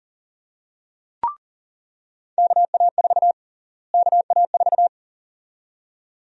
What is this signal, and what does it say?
Morse code, heard as a keyed tone, about 40 wpm, 700 Hz. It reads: KA4 KA4
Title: USA Call Sign Prefixes - Repeated 2x - 40wpm